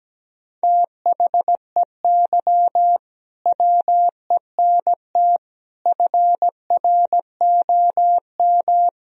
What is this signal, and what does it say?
Morse code, heard as a keyed tone, about 17 wpm, 700 Hz. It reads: THEY WENT FROM
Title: Sentences from Top 1000 Words - Review Part 2 - 17wpm